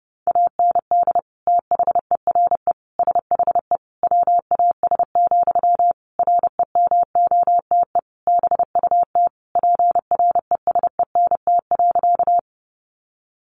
Morse code, heard as a keyed tone, about 30 wpm, 700 Hz. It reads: AND THERE SHE WAS, REMOTE BUT PRESENT.
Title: Sentences from Top 1300 Words 30wpm